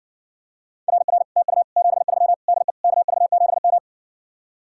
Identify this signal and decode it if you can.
Morse code, heard as a keyed tone, about 60 wpm, 700 Hz. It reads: RR TU 73 DE KV7M